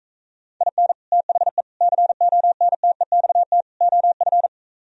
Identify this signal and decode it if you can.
Morse code, heard as a keyed tone, about 44 wpm, 700 Hz. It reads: IN THE CONTEXT OF